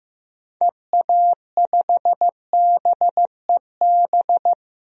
Morse code, heard as a keyed tone, about 15 wpm, 700 Hz. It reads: EA5BEB
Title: DXCC-100 Call Signs - 6 Characters - 15wpm